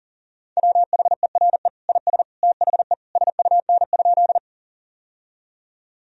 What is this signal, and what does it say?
Morse code, heard as a keyed tone, about 40 wpm, 700 Hz. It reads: WHERE IS THE SUN?
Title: Sentences from Top 1000 Words - Review Part 1 - 40wpm